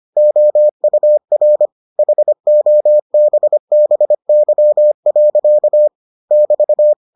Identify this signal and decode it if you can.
Morse code, heard as a keyed tone, about 25 wpm, 600 Hz. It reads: OUR HOBBY. <BT>